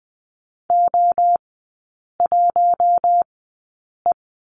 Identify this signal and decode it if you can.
Morse code, heard as a keyed tone, about 20 wpm, spaced as for 8 wpm, 700 Hz. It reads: O1E
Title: 041 Lesson 8e - Number 1 - Three Characters 20wpm